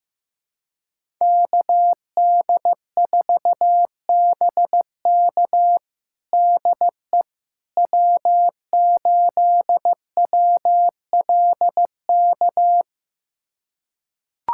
Morse code, heard as a keyed tone, about 15 wpm, 700 Hz. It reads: KD4BK DE W8WLK
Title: USA Callsign DE Callsign 15wpm